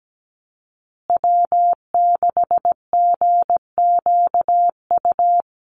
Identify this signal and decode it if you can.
Morse code, heard as a keyed tone, about 17 wpm, 700 Hz. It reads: W6GQU
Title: USA Call Signs Repeated 2x 17wpm